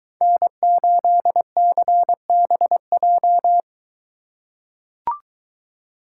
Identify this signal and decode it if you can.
Morse code, heard as a keyed tone, about 23 wpm, 700 Hz. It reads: N8CBJ